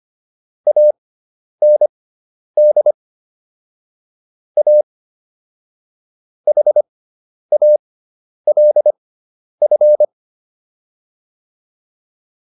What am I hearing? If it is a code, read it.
Morse code, heard as a keyed tone, about 25 wpm, spaced as for 10 wpm, 600 Hz. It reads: AND A HALF